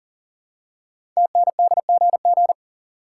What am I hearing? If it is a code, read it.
Morse code, heard as a keyed tone, about 38 wpm, 700 Hz. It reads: TNDGG
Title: Code Groups - 5 Letters - 40wpm